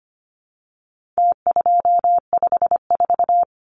Morse code, heard as a keyed tone, about 25 wpm, 700 Hz. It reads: T254
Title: Numbers - Contest Style - 4-Digit 25wpm